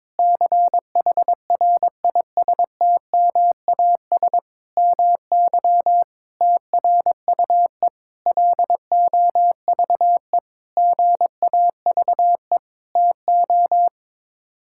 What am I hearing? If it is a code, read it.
Morse code, heard as a keyed tone, about 22 wpm, 700 Hz. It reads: CHRISTMAS MY TRUE LOVE GAVE TO
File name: practice / 22wpm / Sets of 8 Words - 22wpm